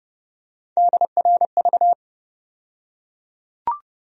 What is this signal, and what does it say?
Morse code, heard as a keyed tone, about 30 wpm, 700 Hz. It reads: DRV
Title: USA Call Sign Postfixes - 3 Letters 30wpm